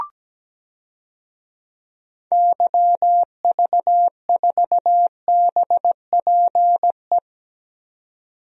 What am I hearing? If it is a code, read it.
Morse code, heard as a keyed tone, about 17 wpm, 700 Hz. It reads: YV4BPE
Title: DXCC-100 Call Signs - 6 Characters - 17wpm